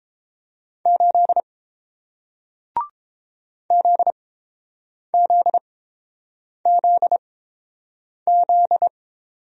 Morse code, heard as a keyed tone, about 30 wpm, spaced as for 7 wpm, 700 Hz. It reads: 8 ZZZZ